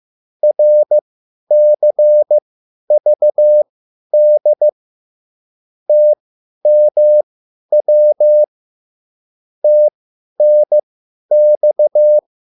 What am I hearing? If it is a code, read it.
Morse code, heard as a keyed tone, about 15 wpm, spaced as for 10 wpm, 600 Hz. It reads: RCVD TMW TNX